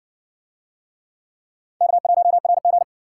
Morse code, heard as a keyed tone, about 53 wpm, 700 Hz. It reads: K1RG